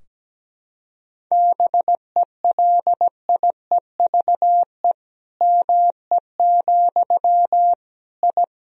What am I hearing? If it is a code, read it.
Morse code, heard as a keyed tone, about 17 wpm, 700 Hz. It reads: BELIEVE ME, I
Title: Sentences from Top 1000 Words - Review Part 2 - 17wpm